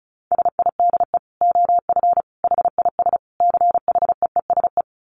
Morse code, heard as a keyed tone, about 34 wpm, 700 Hz. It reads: SIDE OF HIS CHEESE
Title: Dit Challenge 35wpm